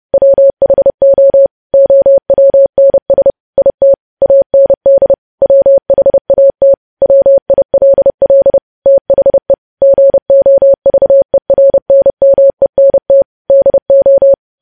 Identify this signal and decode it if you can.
Morse code, heard as a keyed tone, about 30 wpm, 550 Hz. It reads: WHO OWNS IT AND WHAT WILL THE GOVERNMENT DO
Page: news headlines in morse code at 30 wpm